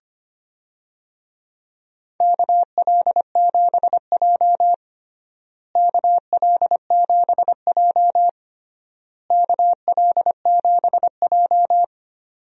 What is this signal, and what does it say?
Morse code, heard as a keyed tone, about 25 wpm, 700 Hz. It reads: KL7J KL7J KL7J
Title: DXCC-100 Call Signs - 4 Characters - Repeated 3x - 25wpm